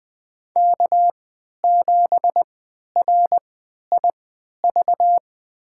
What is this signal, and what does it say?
Morse code, heard as a keyed tone, about 20 wpm, spaced as for 11 wpm, 700 Hz. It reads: K7RIV
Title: USA Call Signs - 5 Characters - 3x Character Spacing - 20wpm